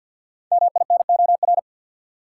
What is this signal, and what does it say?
Morse code, heard as a keyed tone, about 51 wpm, 700 Hz. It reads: MINOR
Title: Sentences from Top 1500 Words - New Word Drill Alt - 50wpm